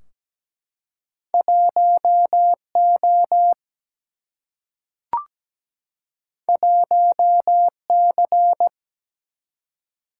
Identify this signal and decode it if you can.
Morse code, heard as a keyed tone, about 17 wpm, 700 Hz. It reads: 1O 1C
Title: Single-Letter Single-Number 17wpm